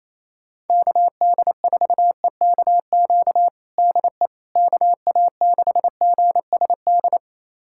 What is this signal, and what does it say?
Morse code, heard as a keyed tone, about 28 wpm, 700 Hz. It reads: KD4EKQ DE KA6GSD
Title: USA Callsign DE Callsign - 6 Characters - 28wpm